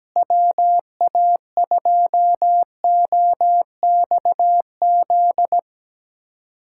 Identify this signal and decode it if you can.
Morse code, heard as a keyed tone, about 17 wpm, 700 Hz. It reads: WA2OXZ
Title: USA Callsign DE Callsign - 6 Characters - 17wpm